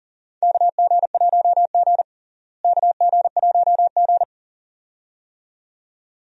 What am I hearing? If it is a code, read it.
Morse code, heard as a keyed tone, about 40 wpm, 700 Hz. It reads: KG1G KG1G
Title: USA Call Signs - 4 Characters - Repeated 2x - 40wpm